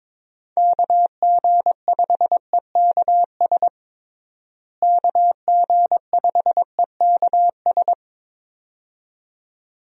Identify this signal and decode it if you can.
Morse code, heard as a keyed tone, about 22 wpm, 700 Hz. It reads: KG5EKS KG5EKS
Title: USA Call Signs - 6 Characters - Repeated 2x - 22wpm